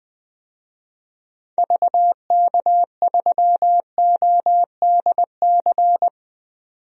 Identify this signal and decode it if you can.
Morse code, heard as a keyed tone, about 20 wpm, 700 Hz. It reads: VK3ODC
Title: DXCC-100 Call Signs - 6 Characters - 20wpm